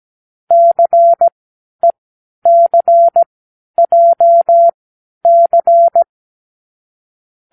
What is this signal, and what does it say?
Morse code, heard as a keyed tone, about 17 wpm, spaced as for 10 wpm, 680 Hz. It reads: CECJC
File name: k7qo / Lesson15(A-J)